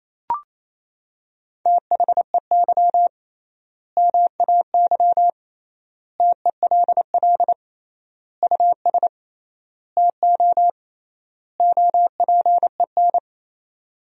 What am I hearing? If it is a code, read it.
Morse code, heard as a keyed tone, about 28 wpm, 700 Hz. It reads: THEY MAY TELL US TO OPEN
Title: 3x Word-Spacing - Sentences from Top 300 Words - 28wpm